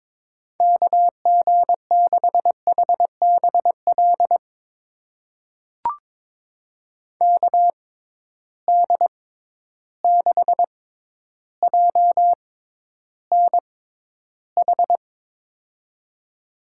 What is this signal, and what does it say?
Morse code, heard as a keyed tone, about 22 wpm, spaced as for 11 wpm, 700 Hz. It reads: KG6HBL KD6JNH
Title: USA Call Signs - 6x Character Spacing - 22wpm